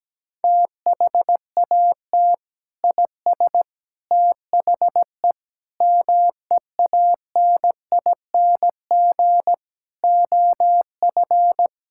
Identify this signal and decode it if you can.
Morse code, heard as a keyed tone, about 17 wpm, 700 Hz. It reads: THAT IS THE MEANING OF
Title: Sentences from Top 1000 Words - Review Part 6 - 17wpm